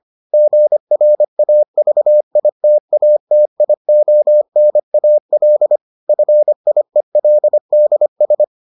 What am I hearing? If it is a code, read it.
Morse code, heard as a keyed tone, about 25 wpm, 600 Hz. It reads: GRAVITATIONAL FIELDS